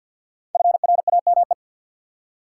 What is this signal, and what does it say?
Morse code, heard as a keyed tone, about 50 wpm, 700 Hz. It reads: FRAME
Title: Five Letter Words - 50wpm